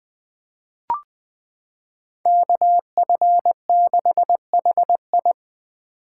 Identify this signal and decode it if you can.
Morse code, heard as a keyed tone, about 20 wpm, 700 Hz. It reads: KF6HI
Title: 2023 SOTA Chaser Call Signs - 20wpm